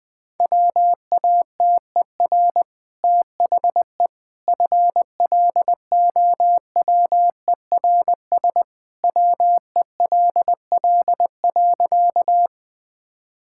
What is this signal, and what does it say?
Morse code, heard as a keyed tone, about 20 wpm, 700 Hz. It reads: WATER THE FLOWERS WELL.
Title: Easy Sentences with Hints - 20wpm